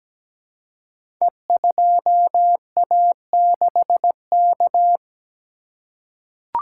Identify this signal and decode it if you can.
Morse code, heard as a keyed tone, about 17 wpm, 700 Hz. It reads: E2A6K
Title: Code Groups - 5 Letter:Numbers - 17wpm